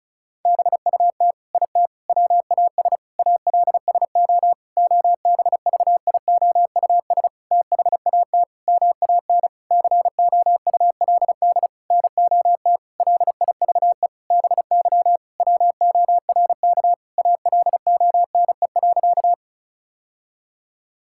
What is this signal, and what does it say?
Morse code, heard as a keyed tone, about 35 wpm, 700 Hz. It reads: BUT IT WAS ALSO OBVIOUS THAT MAN COULD NOT LIVE BY WORK ALONE.
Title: Sentences from Top 1600 Words 35wpm